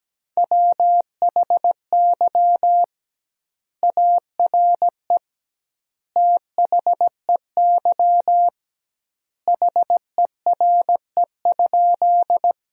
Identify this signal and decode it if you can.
Morse code, heard as a keyed tone, about 17 wpm, 700 Hz. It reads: WHY ARE THEY HERE?
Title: 2x Word-Spacing - Sentences from Top 200 Words - 17wpm